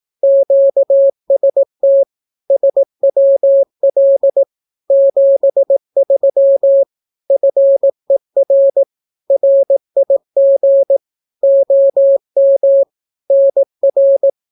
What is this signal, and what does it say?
Morse code, heard as a keyed tone, about 18 wpm, 550 Hz. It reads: QST SWL 73 FER RIG OM NR
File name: naukapodstaw / skroty / skroty